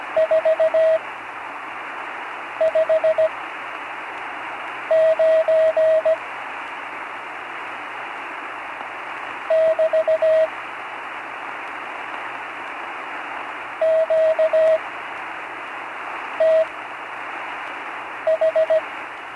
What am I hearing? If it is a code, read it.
Morse code, heard as a keyed tone, about 17 wpm, spaced as for 5 wpm, 640 Hz, with radio noise. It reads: 459 <BT> QTH
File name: CW Learning Programs / Learning Morse Code (from Tune in the World) / 43 on the air